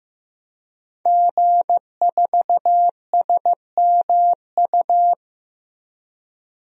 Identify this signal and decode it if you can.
Morse code, heard as a keyed tone, about 15 wpm, 700 Hz. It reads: G4SMU
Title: DXCC-100 Call Signs - 5 Characters - Repeated 3x - 15wpm